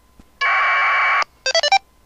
Packet radio (AX.25, as in APRS).